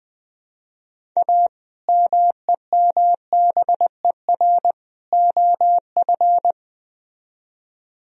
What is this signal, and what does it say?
Morse code, heard as a keyed tone, about 20 wpm, 700 Hz. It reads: A MEMBER OF
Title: Speed-Racing - Sets of 3 Words 17wpm